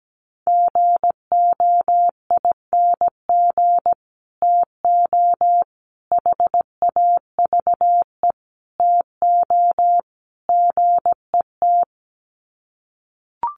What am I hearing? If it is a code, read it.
Morse code, heard as a keyed tone, about 17 wpm, 700 Hz. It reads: GOING TO HAVE TO GET
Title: Sets of 5 Words Encore2 17wpm